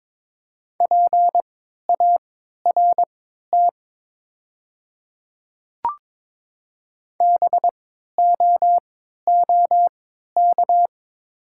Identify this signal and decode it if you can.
Morse code, heard as a keyed tone, about 22 wpm, spaced as for 13 wpm, 700 Hz. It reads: PART BOOK
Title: 3x Character Spacing - Top 300 Words - 22wpm